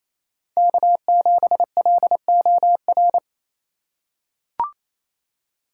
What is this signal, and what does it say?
Morse code, heard as a keyed tone, about 28 wpm, 700 Hz. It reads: K7LOR
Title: USA Call Signs - 5 Characters - 4x Character Spacing - 28wpm